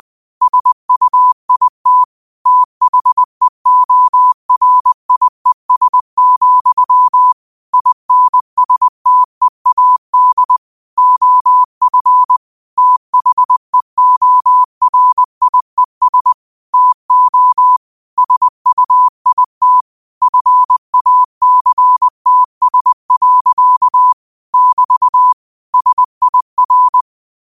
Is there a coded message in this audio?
Yes: Morse code from 0.4 to 27.0 s, heard as a keyed tone, about 20 wpm, 1000 Hz. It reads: SUIT THEORIES, INSTEAD OF THEORIES TO SUIT FACTS. <BT> SIR